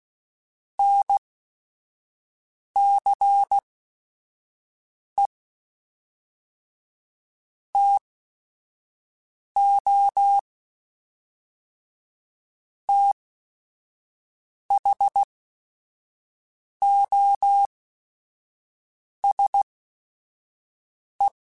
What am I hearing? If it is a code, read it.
Morse code, heard as a keyed tone, about 16 wpm, spaced as for 5 wpm, 800 Hz. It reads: NCE TO THOSE
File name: Technical Subjects / Technical Subjects-mp3s / XC @5